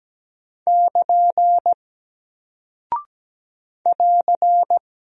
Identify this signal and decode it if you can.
Morse code, heard as a keyed tone, about 17 wpm, 700 Hz. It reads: <KN> <AR>
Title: Single Character (Letter:Number:Punctuation:Prosign) Alt 17wpm